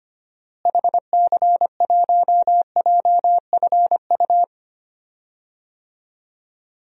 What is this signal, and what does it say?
Morse code, heard as a keyed tone, about 25 wpm, 700 Hz. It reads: HC1JFU
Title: DXCC-100 Call Signs - 6 Characters - Repeated 3x - 25wpm